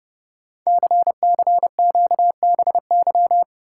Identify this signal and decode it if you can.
Morse code, heard as a keyed tone, about 28 wpm, 700 Hz. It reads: CCQBY
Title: Code Groups - 5 Letters - 30wpm